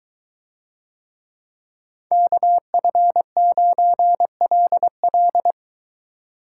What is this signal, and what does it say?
Morse code, heard as a keyed tone, about 23 wpm, 700 Hz. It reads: KF9LL